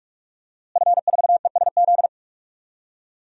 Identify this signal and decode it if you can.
Morse code, heard as a keyed tone, about 46 wpm, 700 Hz. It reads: F4ESZ
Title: DXCC-100 Call Signs - 5 Characters - Repeated 2x - 45wpm